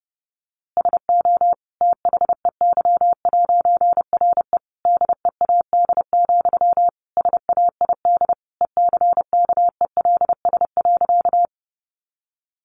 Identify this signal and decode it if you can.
Morse code, heard as a keyed tone, about 30 wpm, 700 Hz. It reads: SO THEY'RE DEAD, SAID ECKELS.
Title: A Sound Of Thunder 30wpm